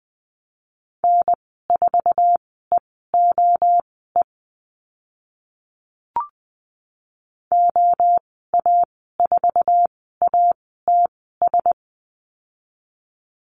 Morse code, heard as a keyed tone, about 20 wpm, spaced as for 15 wpm, 700 Hz. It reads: N4EOE OA4ATS
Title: 062 Lesson 11g - Letter h - Call Signs 20wpm